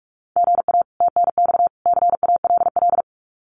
Morse code, heard as a keyed tone, about 44 wpm, 700 Hz. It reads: GA TNX CALL
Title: Common QSO Phrases 45wpm